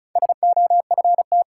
Morse code, heard as a keyed tone, about 34 wpm, 700 Hz. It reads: SOFT
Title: Sentences from Top 1500 Words - New Word Drill Alt - 35wpm